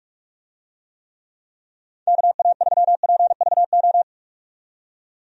Morse code, heard as a keyed tone, about 48 wpm, 700 Hz. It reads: KA3PVO
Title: USA Call Signs - 6 Characters - 45wpm